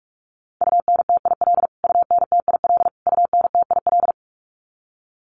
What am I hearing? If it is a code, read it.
Morse code, heard as a keyed tone, about 44 wpm, 700 Hz. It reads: UNTIL UNTIL UNTIL
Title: Top 500 Words Repeated 45wpm